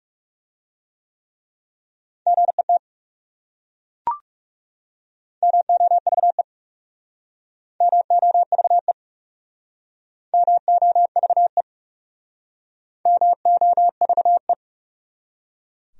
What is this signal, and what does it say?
Morse code, heard as a keyed tone, about 39 wpm, 700 Hz. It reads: GET MOVE MOVE MOVE MOVE